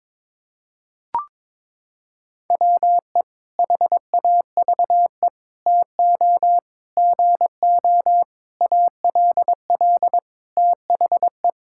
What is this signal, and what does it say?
Morse code, heard as a keyed tone, about 22 wpm, 700 Hz. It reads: WE HAVE TO GO ALL THE
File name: practice / 22wpm / Sentences with 8-Word Phrases - 22wpm